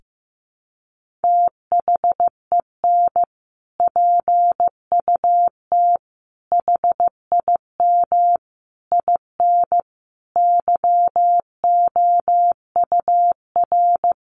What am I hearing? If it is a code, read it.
Morse code, heard as a keyed tone, about 15 wpm, 700 Hz. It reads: THEN PUT HIM IN YOUR
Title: Sentences from Top 1500 Words 15wpm